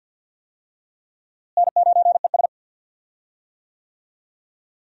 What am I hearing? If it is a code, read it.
Morse code, heard as a keyed tone, about 52 wpm, 700 Hz. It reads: N9ES
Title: USA Call Signs - 4 Characters - Repeated 3x - 50wpm